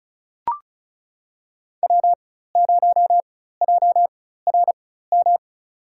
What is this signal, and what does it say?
Morse code, heard as a keyed tone, about 34 wpm, spaced as for 16 wpm, 700 Hz. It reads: W0JRM
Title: USA Call Signs - 4x Character Spacing - 35wpm